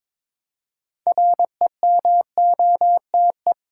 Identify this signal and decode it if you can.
Morse code, heard as a keyed tone, about 22 wpm, 690 Hz. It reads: REMOTE
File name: Difficult Words with Hints - 22wpm